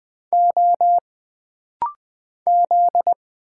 Morse code, heard as a keyed tone, about 20 wpm, 700 Hz. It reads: O Z
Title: Single Letters - Rapid-Fire 20wpm